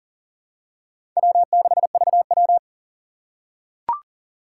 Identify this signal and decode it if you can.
Morse code, heard as a keyed tone, about 40 wpm, 700 Hz. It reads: W6VW
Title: USA Call Signs - 6x Character Spacing - 40wpm